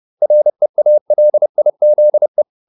Morse code, heard as a keyed tone, about 30 wpm, 600 Hz. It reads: REALIZE